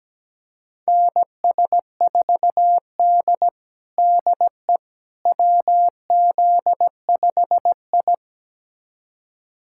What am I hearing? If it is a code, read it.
Morse code, heard as a keyed tone, about 17 wpm, 700 Hz. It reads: NS4D DE WZ5I
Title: USA Callsign DE Callsign - 4 Characters - 17wpm